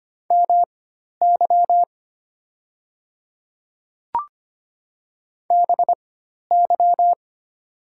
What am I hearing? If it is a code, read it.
Morse code, heard as a keyed tone, about 25 wpm, spaced as for 12 wpm, 700 Hz. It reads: MY BY
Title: 4x Character Spacing - Top 200 Words - 25wpm